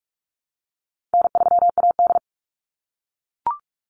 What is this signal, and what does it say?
Morse code, heard as a keyed tone, about 44 wpm, 700 Hz. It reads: N3AD
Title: Call Signs - 45wpm